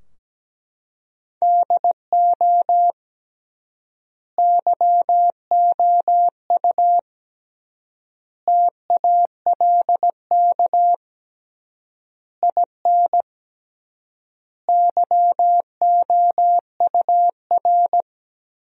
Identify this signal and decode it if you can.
Morse code, heard as a keyed tone, about 17 wpm, 700 Hz. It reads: DO YOU TALK IN YOUR
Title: 3x Word-Spacing - Sentences from Top 500 Words - 17wpm